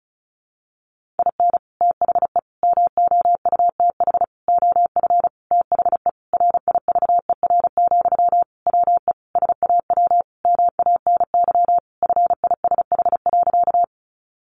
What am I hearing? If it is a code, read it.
Morse code, heard as a keyed tone, about 34 wpm, 700 Hz. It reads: IN THE MOUTH OF THE RIVER, WE SAW MANY FISH.
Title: Sentences from Top 1500 Words 35wpm